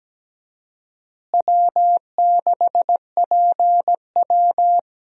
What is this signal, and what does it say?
Morse code, heard as a keyed tone, about 17 wpm, 700 Hz. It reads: W6PW
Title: USA Call Signs - 4 Characters - Repeated 2x - 17wpm